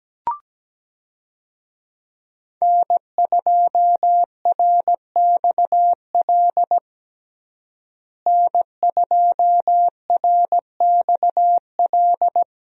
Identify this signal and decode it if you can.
Morse code, heard as a keyed tone, about 17 wpm, 700 Hz. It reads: N2RXL N2RXL
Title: USA Call Signs - 5 Characters - Repeated 2x - 17wpm